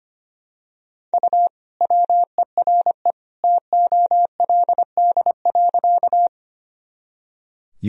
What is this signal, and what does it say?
Morse code, heard as a keyed tone, about 25 wpm, 700 Hz. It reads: U WERE TOLD.
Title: Sentences from Top 400 Words and Abbreviations 25wpm